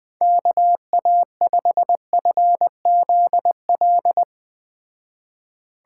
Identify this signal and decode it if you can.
Morse code, heard as a keyed tone, about 19 wpm, 700 Hz. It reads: KA5FZL
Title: Speed Racing - USA 2x3 Call Signs 17wpm